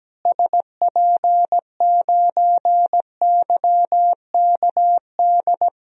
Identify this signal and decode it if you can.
Morse code, heard as a keyed tone, about 17 wpm, 700 Hz. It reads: SP9YKD
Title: DXCC-100 Call Signs - 6 Characters - Repeated 2x - 17wpm